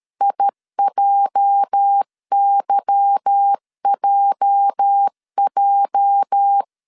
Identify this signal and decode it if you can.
Morse code, heard as a keyed tone, about 13 wpm, 800 Hz. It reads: IJYJJ
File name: LEZIONI / 8 YJ45 / 8 YJ 12wpm 2235